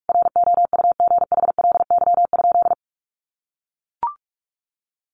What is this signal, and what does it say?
Morse code, heard as a keyed tone, about 44 wpm, 700 Hz. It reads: ROUGHLY?
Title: Sentences from Top 1500 Words 45wpm